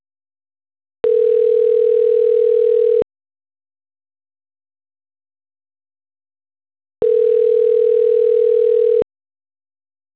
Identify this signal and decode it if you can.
Ringback tone (North America), 440 + 480 Hz.